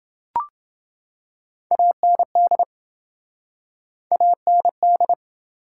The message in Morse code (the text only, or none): AND AND